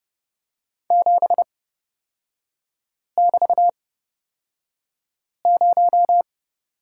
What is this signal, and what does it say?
Morse code, heard as a keyed tone, about 30 wpm, spaced as for 5 wpm, 700 Hz. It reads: 7<BT>0
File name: Lesson 36e - BT - Three Characters 30wpm